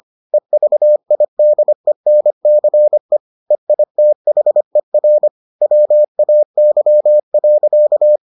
Morse code, heard as a keyed tone, about 25 wpm, 600 Hz. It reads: EVIDENCE EITHER WAY.